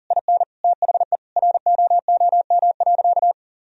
Morse code, heard as a keyed tone, about 40 wpm, 700 Hz. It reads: IN THE ROOM.
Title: Sentences from Top 1000 Words - Review Part 3 - 40wpm